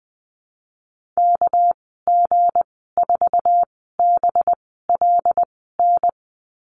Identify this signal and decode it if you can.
Morse code, heard as a keyed tone, about 20 wpm, spaced as for 15 wpm, 700 Hz. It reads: KG4BLN
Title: 231 Lesson 37g - Letter x - Call Signs 20wpm